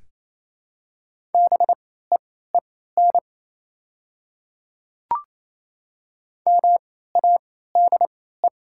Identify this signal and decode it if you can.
Morse code, heard as a keyed tone, about 28 wpm, spaced as for 16 wpm, 700 Hz. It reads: BEEN MADE
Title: 3x Character Spacing - Top 300 Words - 28wpm